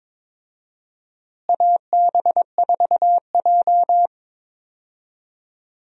Morse code, heard as a keyed tone, about 22 wpm, 700 Hz. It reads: AB4J
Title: USA Call Signs - 4 Characters - Repeated 3x - 22wpm